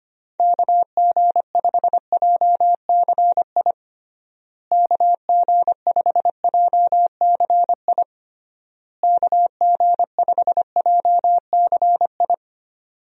Morse code, heard as a keyed tone, about 25 wpm, 700 Hz. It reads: KG5JCI KG5JCI KG5JCI
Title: USA Call Signs - 6 Characters - Repeated 3x - 25wpm